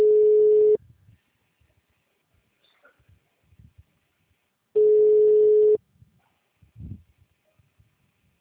Ringback tone (Europe, 425 Hz).
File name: out- -2009- - - .86195